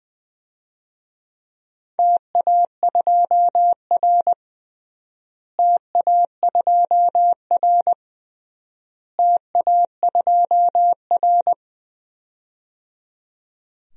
Morse code, heard as a keyed tone, about 20 wpm, 700 Hz. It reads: TA2R TA2R TA2R
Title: DXCC-100 Call Signs - 4 Characters - Repeated 3x - 20wpm